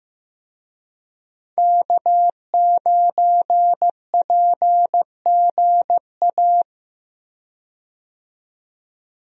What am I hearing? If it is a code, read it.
Morse code, heard as a keyed tone, about 16 wpm, 700 Hz. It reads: K9PGA
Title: USA Call Signs - 5 Characters - 15wpm